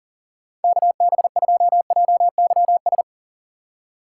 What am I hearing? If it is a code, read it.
Morse code, heard as a keyed tone, about 40 wpm, 700 Hz. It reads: KB2JYS
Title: USA Call Signs - 6 Characters - 6x Character Spacing - 40wpm